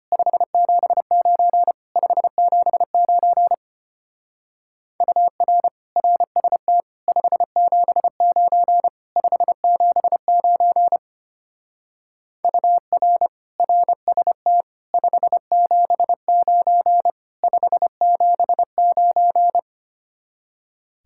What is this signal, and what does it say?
Morse code, heard as a keyed tone, about 31 wpm, 700 Hz. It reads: 579 579 UR RST 579 579 UR RST 579 579
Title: Speed-Racing Common QSO Phrases - 25wpm